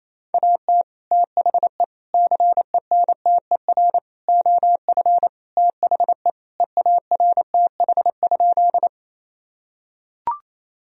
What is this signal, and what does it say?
Morse code, heard as a keyed tone, about 28 wpm, 700 Hz. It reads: AT THE CENTER OF THE EARTH?
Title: Sentences from Top 1000 Words - Review Part 2 - 28wpm